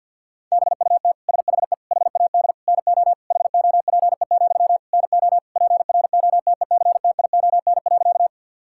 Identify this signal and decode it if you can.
Morse code, heard as a keyed tone, about 50 wpm, 700 Hz. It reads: BUT SHE HAD NO HOPE, NO PROTECTION.
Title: Sentences from Top 1600 Words 50wpm